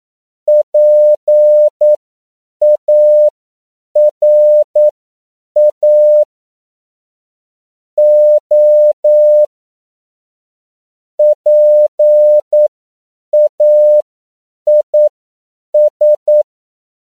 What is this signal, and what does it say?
Morse code, heard as a keyed tone, about 9 wpm, spaced as for 7 wpm, 600 Hz. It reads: PARA O PAIS